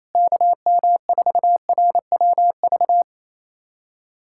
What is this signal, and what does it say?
Morse code, heard as a keyed tone, about 28 wpm, 700 Hz. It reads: KM4RWV
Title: USA Callsign DE Callsign 28wpm